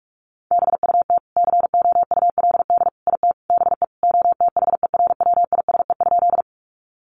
Morse code, heard as a keyed tone, about 44 wpm, 700 Hz. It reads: BUT COULD IT BE OTHERWISE?
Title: Sentences from Top 1200 Words 45wpm